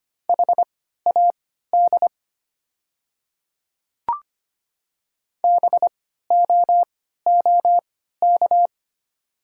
Morse code, heard as a keyed tone, about 25 wpm, spaced as for 14 wpm, 700 Hz. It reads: HAD BOOK
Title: 3x Character Spacing - Top 300 Words - 25wpm